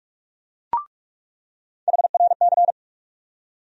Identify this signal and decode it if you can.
Morse code, heard as a keyed tone, about 46 wpm, 700 Hz. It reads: HRC
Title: USA Call Sign Postfixes - 3 Letters 45wpm